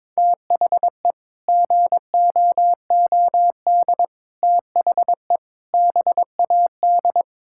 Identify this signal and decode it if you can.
Morse code, heard as a keyed tone, about 22 wpm, 700 Hz. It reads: THE GOOD THE BAD
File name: practice / 22wpm / Sets of 7 Words - 22wpm